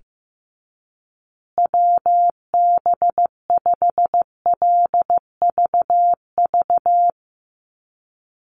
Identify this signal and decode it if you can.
Morse code, heard as a keyed tone, about 15 wpm, 700 Hz. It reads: WB5LVV
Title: USA Call Signs Repeated 2x 15wpm